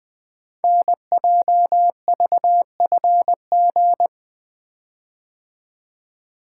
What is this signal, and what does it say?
Morse code, heard as a keyed tone, about 20 wpm, 700 Hz. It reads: NJVFG
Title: Code Groups - 5 Letters - 20wpm